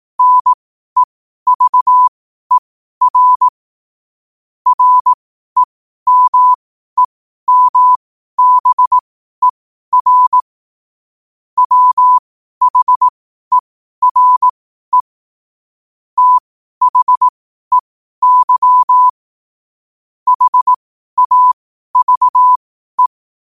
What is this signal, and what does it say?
Morse code, heard as a keyed tone, about 18 wpm, spaced as for 12 wpm, 1000 Hz. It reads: NEVER REMEMBER WHERE THEY HAVE
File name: qotdfiles / 12wpm